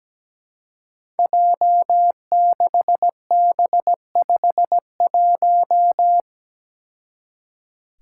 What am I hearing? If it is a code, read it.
Morse code, heard as a keyed tone, about 17 wpm, 700 Hz. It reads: J6B51
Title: Five Challenging Letter-Number Combinations 17wpm